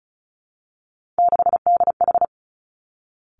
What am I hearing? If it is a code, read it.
Morse code, heard as a keyed tone, about 36 wpm, 700 Hz. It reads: 6DH
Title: Three Challenging Letter-Number Combinations 35wpm